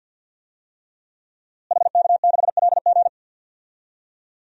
Morse code, heard as a keyed tone, about 46 wpm, 700 Hz. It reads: HK6LG
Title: DXCC-100 Call Signs - 5 Characters - Repeated 3x - 50wpm